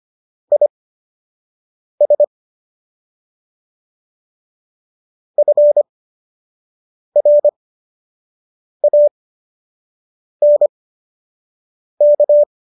Morse code, heard as a keyed tone, about 25 wpm, spaced as for 6 wpm, 590 Hz. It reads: IS FRANK